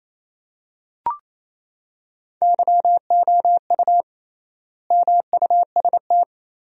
Morse code, heard as a keyed tone, about 27 wpm, 700 Hz. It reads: YOU MUST
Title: 3x Word-Spacing - Sentences from Top 300 Words - 28wpm